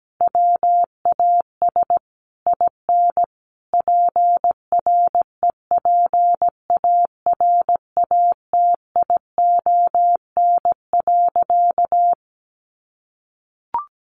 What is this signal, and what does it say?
Morse code, heard as a keyed tone, about 17 wpm, 700 Hz. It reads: WAS IN PREPARATION.
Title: Sentences from Top 1400 Words 17wpm